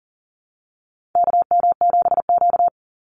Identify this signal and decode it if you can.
Morse code, heard as a keyed tone, about 40 wpm, 700 Hz. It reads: KM7Q
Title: Call Signs Repeated 2x - 40wpm